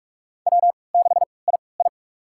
Morse code, heard as a keyed tone, about 42 wpm, spaced as for 25 wpm, 700 Hz. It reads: W6II
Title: USA Call Signs - 3x Character Spacing - 45wpm